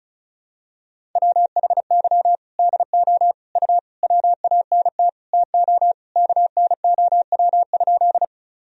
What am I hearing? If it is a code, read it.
Morse code, heard as a keyed tone, about 34 wpm, 690 Hz. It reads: WHY DO U WANT TO KNOW?
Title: Sentences from Top 200 Words and Abbreviations 35wpm